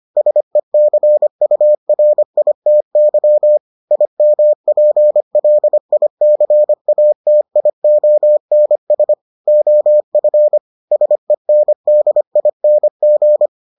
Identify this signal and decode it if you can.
Morse code, heard as a keyed tone, about 25 wpm, 600 Hz. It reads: SECURITY IMPLICATIONS OF SENDING